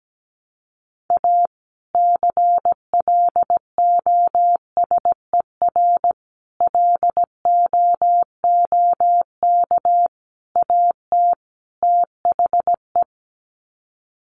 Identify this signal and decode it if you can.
Morse code, heard as a keyed tone, about 17 wpm, 700 Hz. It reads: A CLOSER LOOK AT THE
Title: Sets of 5 Words Encore 17wpm